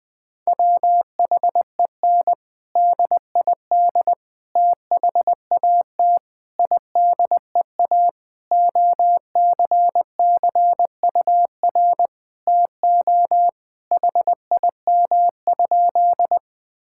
Morse code, heard as a keyed tone, about 21 wpm, 700 Hz. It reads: WHEN DID THAT IDEA OCCUR TO HIM?
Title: Sentences from Top 1000 Words - Review Part 6 - 20wpm